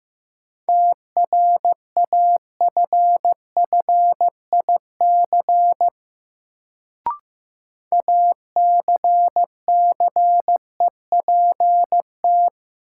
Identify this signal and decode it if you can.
Morse code, heard as a keyed tone, about 15 wpm, 700 Hz. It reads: TRAFFIC ACCEPT
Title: Sentences from Top 1300 Words - New Word Drill - 15wpm